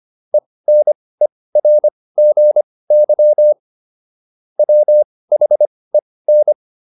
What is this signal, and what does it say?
Morse code, heard as a keyed tone, about 25 wpm, spaced as for 18 wpm, 600 Hz. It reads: ENERGY WHEN